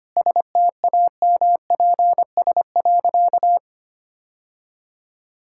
Morse code, heard as a keyed tone, about 25 wpm, 690 Hz. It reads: STAMPS.
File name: Easy Sentences with Hints - 25wpm